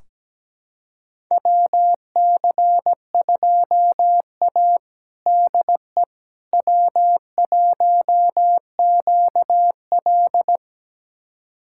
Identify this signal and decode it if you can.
Morse code, heard as a keyed tone, about 17 wpm, 700 Hz. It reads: WC2A DE W1QL
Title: USA Callsign DE Callsign - 4 Characters - 17wpm